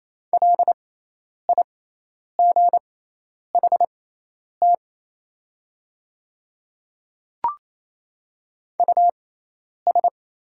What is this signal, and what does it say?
Morse code, heard as a keyed tone, about 28 wpm, spaced as for 10 wpm, 700 Hz. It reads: LIGHT US
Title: 6x Character Spacing - Top 300 Words - 28wpm